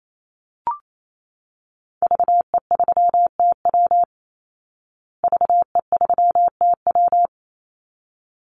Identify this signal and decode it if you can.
Morse code, heard as a keyed tone, about 28 wpm, 700 Hz. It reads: VE3TW VE3TW
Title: Call Signs Repeated 3x - 28wpm